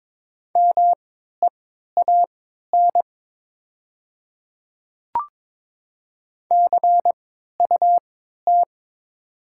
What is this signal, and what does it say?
Morse code, heard as a keyed tone, about 22 wpm, spaced as for 13 wpm, 700 Hz. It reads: MEAN CUT
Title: 3x Character Spacing - Top 300 Words - 22wpm